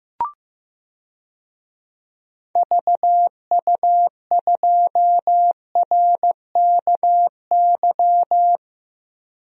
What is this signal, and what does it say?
Morse code, heard as a keyed tone, about 15 wpm, 700 Hz. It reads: VU2RKY
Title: DXCC-100 Call Signs - 6 Characters - Repeated 3x - 15wpm